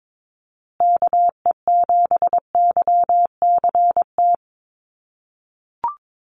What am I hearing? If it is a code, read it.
Morse code, heard as a keyed tone, about 22 wpm, 700 Hz. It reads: KE7YCT
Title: USA Call Signs Repeated 2x 22wpm